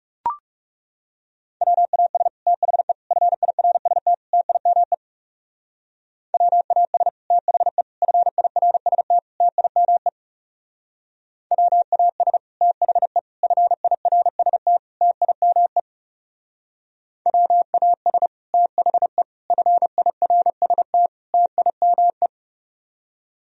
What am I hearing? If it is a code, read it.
Morse code, heard as a keyed tone, about 38 wpm, 700 Hz. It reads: WAS THE FIRST TIME WAS THE FIRST TIME WAS THE FIRST TIME WAS THE FIRST TIME